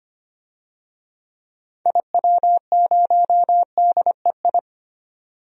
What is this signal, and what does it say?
Morse code, heard as a keyed tone, about 25 wpm, 700 Hz. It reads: IW0DEI
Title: DXCC-100 Call Signs - 6 Characters - 25wpm